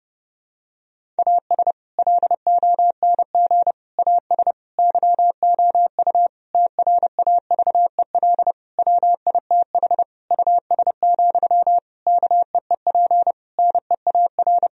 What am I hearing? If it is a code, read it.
Morse code, heard as a keyed tone, about 30 wpm, 700 Hz. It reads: AS LONG AS YOU TRAVEL WITH US, KEEP NEAR
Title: Sentences from Top 1000 Words - Review Part 2 - 30wpm